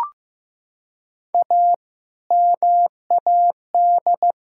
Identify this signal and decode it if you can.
Morse code, heard as a keyed tone, about 15 wpm, 700 Hz. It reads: A MAD
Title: Pangram Challenge v2 - 15wpm